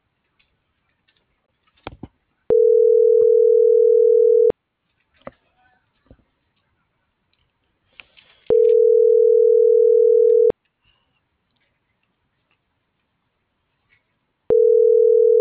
Ringback tone (North America), 440 + 480 Hz.